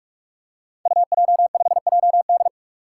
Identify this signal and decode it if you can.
Morse code, heard as a keyed tone, about 44 wpm, 700 Hz. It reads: UJ5JD
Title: Code Groups - 5 Letter:Numbers - 45wpm